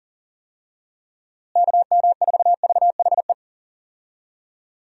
Morse code, heard as a keyed tone, about 40 wpm, 700 Hz. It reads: KM4VHE
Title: USA Call Signs - 6 Characters - Repeated 2x - 40wpm